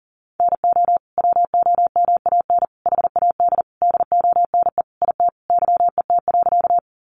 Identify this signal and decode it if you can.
Morse code, heard as a keyed tone, about 40 wpm, 700 Hz. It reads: NO WOMAN HAD DONE IT YET.
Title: Sentences from Top 1100 Words 40wpm